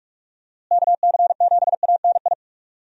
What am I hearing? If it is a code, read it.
Morse code, heard as a keyed tone, about 47 wpm, 700 Hz. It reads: KC7ANI
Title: USA Call Signs - 6 Characters - Repeated 3x - 45wpm